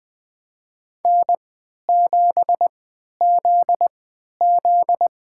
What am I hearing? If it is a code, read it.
Morse code, heard as a keyed tone, about 20 wpm, spaced as for 11 wpm, 700 Hz. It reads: N7ZZ
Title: USA Call Signs - 4 Characters - 3x Character Spacing - 20wpm